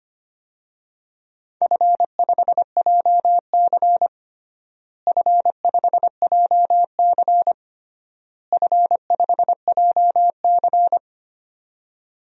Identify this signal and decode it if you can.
Morse code, heard as a keyed tone, about 25 wpm, 700 Hz. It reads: F5JC F5JC F5JC
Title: DXCC-100 Call Signs - 4 Characters - Repeated 3x - 25wpm